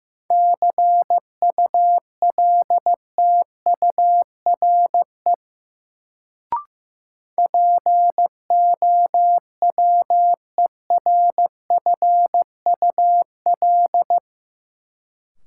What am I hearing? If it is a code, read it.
Morse code, heard as a keyed tone, about 15 wpm, 700 Hz. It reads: CULTURE POWERFUL
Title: Sentences from Top 1000 Words - Review Part 6 - New Word Drill - 15wpm